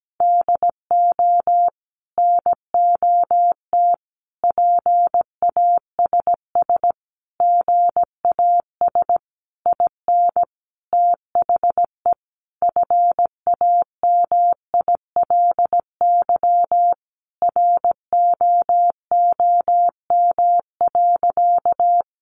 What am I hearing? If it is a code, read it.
Morse code, heard as a keyed tone, about 17 wpm, 700 Hz. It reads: DO NOT PASS GAS IN THE FAMILY ROOM.
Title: Sentences from Top 1100 Words 17wpm